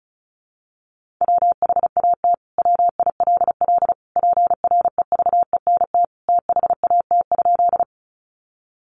Morse code, heard as a keyed tone, about 34 wpm, 700 Hz. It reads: WHAT WILL PREVENT THAT?
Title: Sentences from Top 1100 Words 35wpm